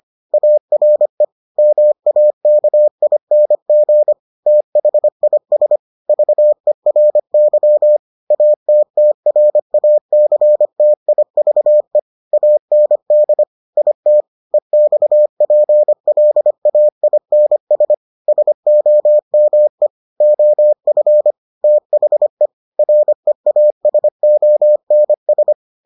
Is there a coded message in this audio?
Yes: Morse code from 0.3 to 25.5 s, heard as a keyed tone, about 25 wpm, 600 Hz. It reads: ARE MAKING THIS VERY ATTRACTIVE AND IT EXPLAINS SOME OF THE REASONS